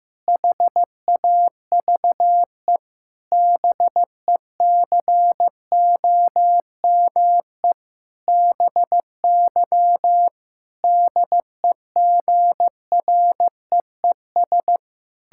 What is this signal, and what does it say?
Morse code, heard as a keyed tone, about 15 wpm, 700 Hz. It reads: HAVE BECOME BY DEGREES
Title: Pangram Challenge v2 - 15wpm